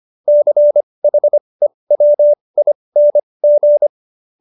Morse code, heard as a keyed tone, about 25 wpm, spaced as for 20 wpm, 590 Hz. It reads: CHEWING